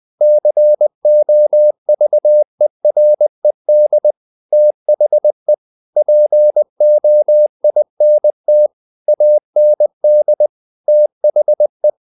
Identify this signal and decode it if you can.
Morse code, heard as a keyed tone, about 20 wpm, 600 Hz. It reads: COVERED THE POINT AND THE
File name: the old man and the sea / the old man and the sea part 1 20wpm